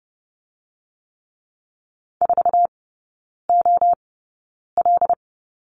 Morse code, heard as a keyed tone, about 30 wpm, spaced as for 9 wpm, 700 Hz. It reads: 4OL
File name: Lesson 16e - Letter c - Three Characters 30wpm